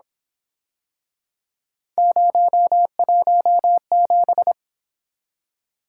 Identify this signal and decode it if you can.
Morse code, heard as a keyed tone, about 27 wpm, 700 Hz. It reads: 017
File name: Speed-Racing - Three Letter-Number Combinations - 22wpm